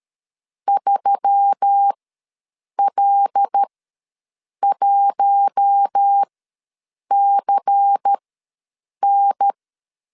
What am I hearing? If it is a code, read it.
Morse code, heard as a keyed tone, about 13 wpm, spaced as for 7 wpm, 800 Hz. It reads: 3L1CN
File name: LEZIONI / 5 LBVC / 5 LBVC ws 12wpm 3159